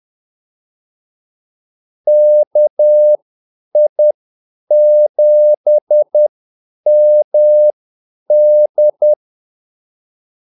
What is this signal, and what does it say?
Morse code, heard as a keyed tone, about 10 wpm, spaced as for 8 wpm, 600 Hz. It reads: KI7MD
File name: Callsigns-10-8wpm